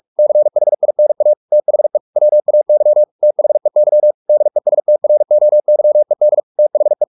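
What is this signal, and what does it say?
Morse code, heard as a keyed tone, about 44 wpm, 600 Hz. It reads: CHINA THE WAY THEY DESTROYED THE
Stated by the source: Morse practice at 45 wpm, the novel Never by Ken Follett in English, from CW med Gnister.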